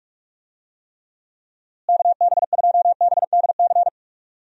Morse code, heard as a keyed tone, about 45 wpm, 700 Hz. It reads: KB2BDC
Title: USA Call Signs - 6 Characters - Repeated 2x - 45wpm